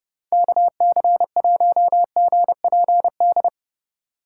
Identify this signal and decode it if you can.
Morse code, heard as a keyed tone, about 29 wpm, 700 Hz. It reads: KC1GPD